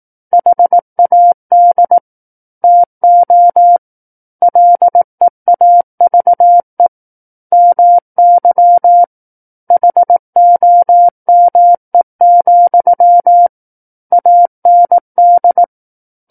Morse code, heard as a keyed tone, about 18 wpm, 700 Hz. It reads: HAD TO LEAVE MY HOME, AND
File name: 18 WPM